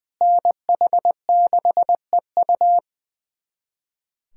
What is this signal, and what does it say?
Morse code, heard as a keyed tone, about 20 wpm, 700 Hz. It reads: NH6EU
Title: USA Callsign DE Callsign - 5 Characters - 20wpm